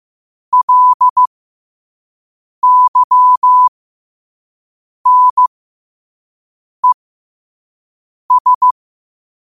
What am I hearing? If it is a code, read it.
Morse code, heard as a keyed tone, about 15 wpm, spaced as for 5 wpm, 1000 Hz. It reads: LYNES
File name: qotdfiles / 5wpm